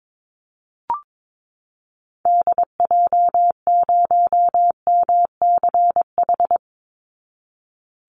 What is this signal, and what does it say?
Morse code, heard as a keyed tone, about 22 wpm, 700 Hz. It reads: DJ0MCH
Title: Call Signs Repeated 2x - 22wpm